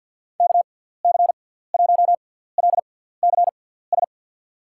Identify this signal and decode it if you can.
Morse code, heard as a keyed tone, about 49 wpm, spaced as for 17 wpm, 700 Hz. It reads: KC1LCS
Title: USA Call Signs - 6 Characters - 6x Character Spacing - 50wpm